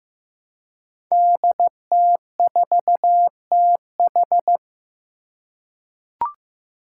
Morse code, heard as a keyed tone, about 15 wpm, 700 Hz. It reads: DT4TH
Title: Five Challenging Letter-Number Combinations 15wpm